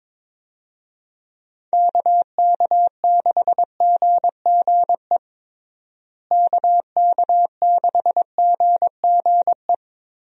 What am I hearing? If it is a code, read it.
Morse code, heard as a keyed tone, about 22 wpm, 700 Hz. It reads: KK6GGE KK6GGE
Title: USA Call Signs - 6 Characters - Repeated 2x - 22wpm